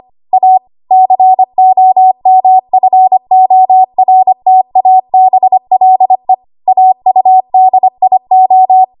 Morse code, heard as a keyed tone, about 25 wpm, 750 Hz. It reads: A COMFORTABLE AUDIO
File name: Archive / 25 WPM / 25WPM